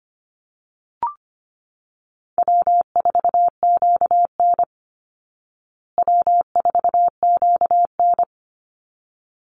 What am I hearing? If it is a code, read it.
Morse code, heard as a keyed tone, about 25 wpm, 700 Hz. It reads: W4QN W4QN
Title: Easy Call Signs Repeated 2x - 25wpm